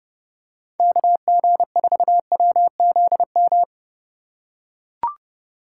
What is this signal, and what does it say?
Morse code, heard as a keyed tone, about 30 wpm, 700 Hz. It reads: KG4WZM
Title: USA Call Signs - 6 Characters - 3x Character Spacing - 30wpm